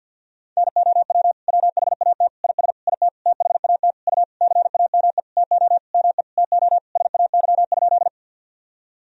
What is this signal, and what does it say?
Morse code, heard as a keyed tone, about 50 wpm, 700 Hz. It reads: NOW WHAT IS IT THAT U CAME TO ME TO SAY?
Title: Sentences from Top 200 Words and Abbreviations 50wpm